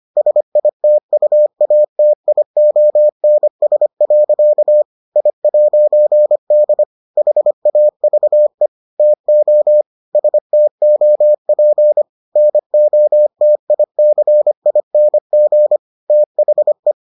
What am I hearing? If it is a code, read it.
Morse code, heard as a keyed tone, about 25 wpm, 600 Hz. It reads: SITUATIONS. I'D HAVE TO STOP NOTICING THE